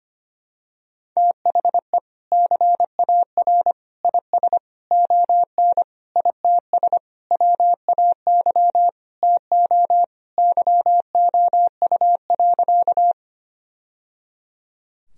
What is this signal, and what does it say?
Morse code, heard as a keyed tone, about 25 wpm, 700 Hz. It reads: THE CAR IS ON ITS WAY TO YOU.
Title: Sentences from Top 1000 Words - Review Part 1 - 25wpm